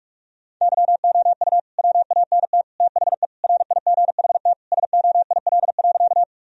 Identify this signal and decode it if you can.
Morse code, heard as a keyed tone, about 41 wpm, 700 Hz. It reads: YOU WANT THE RIGHT SOIL.
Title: Sentences from Top 1000 Words - Review Part 4 - 45wpm